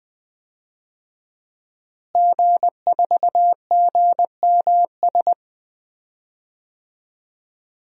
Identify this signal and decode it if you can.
Morse code, heard as a keyed tone, about 20 wpm, 700 Hz. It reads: G4GMS
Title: DXCC-100 Call Signs - 5 Characters - Repeated 3x - 20wpm